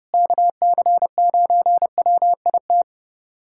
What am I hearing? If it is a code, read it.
Morse code, heard as a keyed tone, about 32 wpm, 700 Hz. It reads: KC9WIT
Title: USA Callsign DE Callsign 30wpm